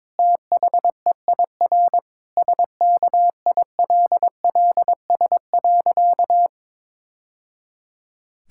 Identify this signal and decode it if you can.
Morse code, heard as a keyed tone, about 22 wpm, 700 Hz. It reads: THEIR SKILLS.
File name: practice / 22wpm / Sentences with 2-Word Phrases - 22wpm